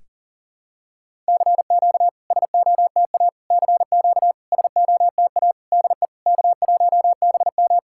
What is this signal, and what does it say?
Morse code, heard as a keyed tone, about 40 wpm, 690 Hz. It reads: CQ SOTA CQ SOTA DE K1BM
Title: SOTA QSO Practice 40wpm